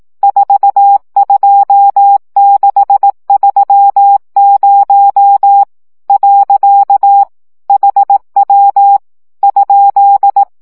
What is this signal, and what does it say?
Morse code, heard as a keyed tone, about 18 wpm, 800 Hz. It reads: 42630 . HW ?
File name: morze vizsga 2011-3 15